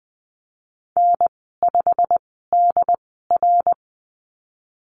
Morse code, heard as a keyed tone, about 20 wpm, spaced as for 15 wpm, 700 Hz. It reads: N5DR
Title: 088 Lesson 15g - Number 5 - Call Signs 20wpm